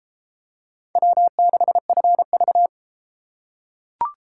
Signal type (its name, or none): Morse code